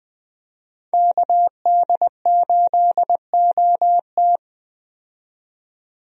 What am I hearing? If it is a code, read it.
Morse code, heard as a keyed tone, about 20 wpm, 700 Hz. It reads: KD8OT
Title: USA Call Signs - 5 Characters - 20wpm